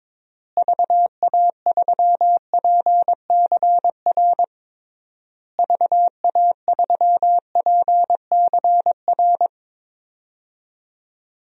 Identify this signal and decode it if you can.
Morse code, heard as a keyed tone, about 22 wpm, 700 Hz. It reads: VA3PCR VA3PCR
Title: DXCC-100 Call Signs - 6 Characters - Repeated 3x - 22wpm